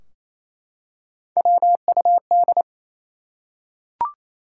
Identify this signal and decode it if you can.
Morse code, heard as a keyed tone, about 27 wpm, 700 Hz. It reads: WUD
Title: Top 180 QSO Elements 28wpm